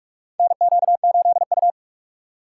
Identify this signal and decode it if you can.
Morse code, heard as a keyed tone, about 43 wpm, 700 Hz. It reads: NQ8U
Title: USA Call Signs - 4 Characters - 3x Character Spacing - 45wpm